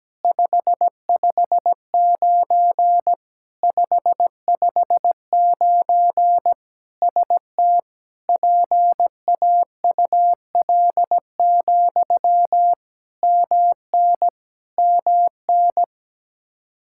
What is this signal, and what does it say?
Morse code, heard as a keyed tone, about 17 wpm, 700 Hz. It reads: 559 559 ST PAUL, MN MN
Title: Practice QSO - Exchange 1 - Beginning - 17wpm